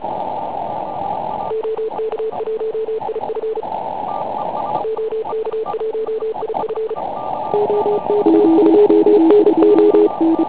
Morse code, heard as a keyed tone, about 34 wpm, 440 Hz, with radio noise. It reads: OK1IF OK1IF OK2NO
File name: Pile UP CUC204